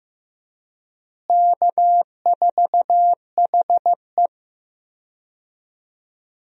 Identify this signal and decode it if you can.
Morse code, heard as a keyed tone, about 15 wpm, 700 Hz. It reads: K4HE
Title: USA Call Signs - 4 Characters - Repeated 2x - 15wpm